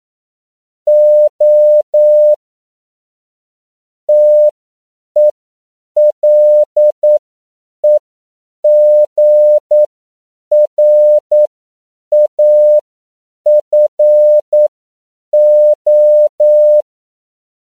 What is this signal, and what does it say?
Morse code, heard as a keyed tone, about 9 wpm, spaced as for 7 wpm, 600 Hz. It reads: O TELEGRAFO
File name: CW / 10PPM